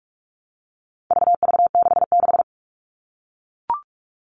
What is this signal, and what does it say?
Morse code, heard as a keyed tone, about 44 wpm, 700 Hz. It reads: VV66
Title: All Cut Numbers - 4-Digit 45wpm